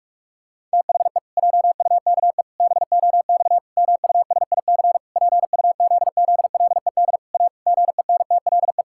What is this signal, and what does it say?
Morse code, heard as a keyed tone, about 46 wpm, 700 Hz. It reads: THE JUKE BOX MUSIC PUZZLED A GENTLE
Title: Pangram Challenge v2 - 45wpm